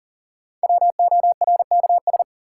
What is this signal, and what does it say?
Morse code, heard as a keyed tone, about 40 wpm, 700 Hz. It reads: WORKS
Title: Sentences from Top 1000 Words - Review Part 5 - New Word Drill - 40wpm